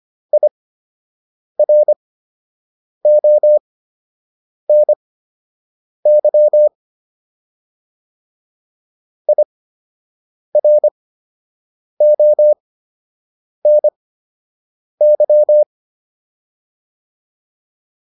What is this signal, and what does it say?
Morse code, heard as a keyed tone, about 25 wpm, spaced as for 7 wpm, 600 Hz. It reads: IRONY IRONY